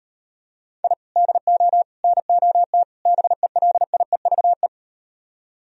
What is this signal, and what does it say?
Morse code, heard as a keyed tone, about 38 wpm, 700 Hz. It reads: I DO NOT BELIEVE